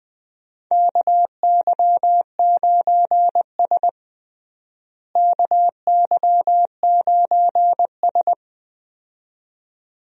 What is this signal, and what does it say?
Morse code, heard as a keyed tone, about 21 wpm, 700 Hz. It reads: KY9S KY9S
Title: USA Call Signs - 4 Characters - Repeated 2x - 20wpm